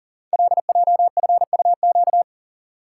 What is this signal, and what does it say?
Morse code, heard as a keyed tone, about 41 wpm, 700 Hz. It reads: LJFUQ
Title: Code Groups - 5 Letters - 40wpm